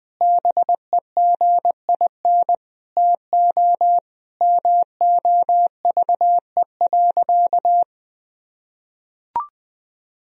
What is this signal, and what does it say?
Morse code, heard as a keyed tone, about 20 wpm, 700 Hz. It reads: BEGIN TO MOVE.
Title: Sentences from Top 1000 Words - Review Part 2 - 20wpm